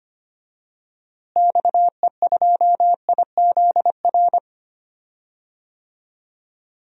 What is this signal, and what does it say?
Morse code, heard as a keyed tone, about 25 wpm, 700 Hz. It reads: XE2IZR
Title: DXCC-100 Call Signs - 6 Characters - Repeated 2x - 25wpm